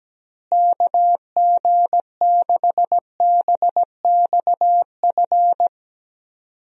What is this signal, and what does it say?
Morse code, heard as a keyed tone, about 17 wpm, 700 Hz. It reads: KG6BXF
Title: USA Call Signs - 6 Characters - Repeated 3x - 17wpm